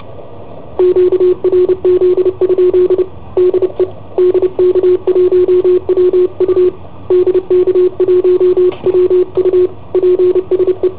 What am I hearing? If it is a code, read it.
Morse code, heard as a keyed tone, about 30 wpm, 360 Hz, with radio noise. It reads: QRZ? DE DK1WU DK1WU PSE
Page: Cassetta esercizi preparata da I7OHP per IW5ELP